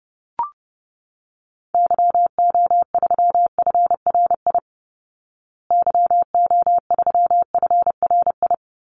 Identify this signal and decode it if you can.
Morse code, heard as a keyed tone, about 30 wpm, 700 Hz. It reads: YO3FRI YO3FRI
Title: Call Signs Repeated 3x - 30wpm